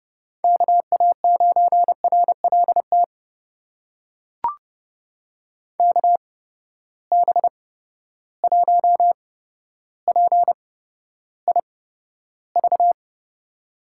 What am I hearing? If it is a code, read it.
Morse code, heard as a keyed tone, about 31 wpm, spaced as for 13 wpm, 700 Hz. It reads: KA9RLT KB1PIV
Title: USA Call Signs - 6 Characters - 8x Character Spacing - 30wpm